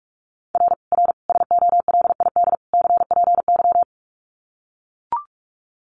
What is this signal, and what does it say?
Morse code, heard as a keyed tone, about 44 wpm, 700 Hz. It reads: R R SOLID CPY
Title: Common QSO Phrases 45wpm